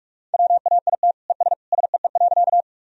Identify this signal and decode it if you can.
Morse code, heard as a keyed tone, about 46 wpm, 700 Hz. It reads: WAIT ES SEE.
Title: Sentences from Top 500 Words and Abbreviations 45wpm